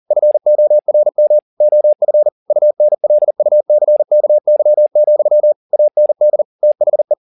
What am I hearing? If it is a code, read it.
Morse code, heard as a keyed tone, about 40 wpm, 600 Hz. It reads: FORM OF UNLUCKY, AND THE
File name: the old man and the sea 1 40wpm